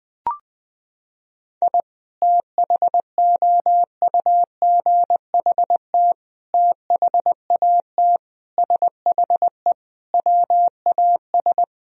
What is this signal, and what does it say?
Morse code, heard as a keyed tone, about 20 wpm, 700 Hz. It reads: I THOUGHT THAT SHE WAS
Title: Sentences from Top 1000 Words - Review Part 3 - 20wpm